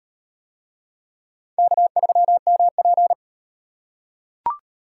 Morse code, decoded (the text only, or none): K3MP